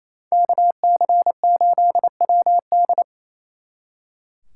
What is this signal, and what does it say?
Morse code, heard as a keyed tone, about 28 wpm, 700 Hz. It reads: KC8WD
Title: Speed Racing - USA 2x2 Call Signs 28wpm